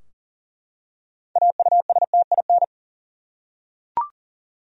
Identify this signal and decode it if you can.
Morse code, heard as a keyed tone, about 40 wpm, 700 Hz. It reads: AUSTIN
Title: USA Common Names - 40wpm